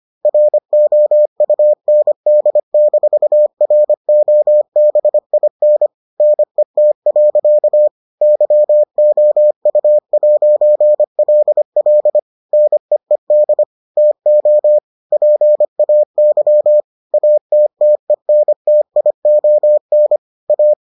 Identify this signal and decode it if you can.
Morse code, heard as a keyed tone, about 25 wpm, 600 Hz. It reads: ROUND-ROBIN NET. YOU'LL NEED TO PAY ATTENTION A